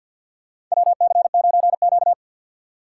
Morse code, heard as a keyed tone, about 51 wpm, 700 Hz. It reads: WC9Q
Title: USA Call Signs - 4x Character Spacing - 50wpm